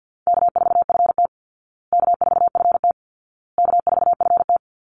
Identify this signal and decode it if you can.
Morse code, heard as a keyed tone, about 50 wpm, 700 Hz. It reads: K4FT K4FT K4FT
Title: Call Signs Repeated 2x - 50wpm